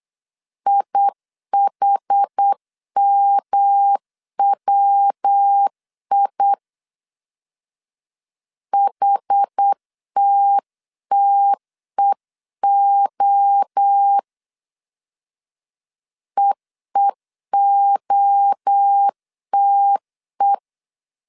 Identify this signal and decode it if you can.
Morse code, heard as a keyed tone, about 8 wpm, 800 Hz. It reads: IHMWI HTTEO EEOTE
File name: CW / DET / 1 3 1500C S C 8wpm 4214